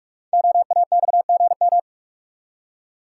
Morse code, heard as a keyed tone, about 45 wpm, 700 Hz. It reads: OAXGM
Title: Code Groups - 5 Letters - 45wpm